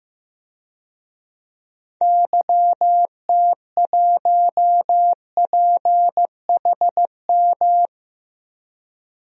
Morse code, heard as a keyed tone, about 15 wpm, 700 Hz. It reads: YT1PHM
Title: DXCC-100 Call Signs - 6 Characters - 15wpm